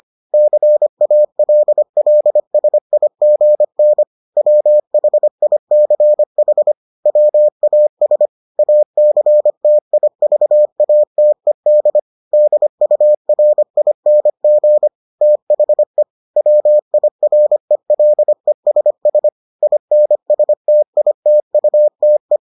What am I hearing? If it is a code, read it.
Morse code, heard as a keyed tone, about 25 wpm, 600 Hz. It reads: CALLSIGN WHICH WAS ACTIVATED DURING THE WIRELESS INSTITUTE